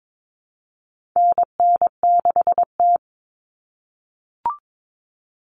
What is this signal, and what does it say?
Morse code, heard as a keyed tone, about 22 wpm, 700 Hz. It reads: NN6T
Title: Call Signs - 22wpm